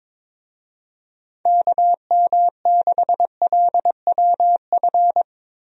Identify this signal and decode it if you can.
Morse code, heard as a keyed tone, about 21 wpm, 700 Hz. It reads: KM6LWF
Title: USA Call Signs - 6 Characters - Repeated 2x - 22wpm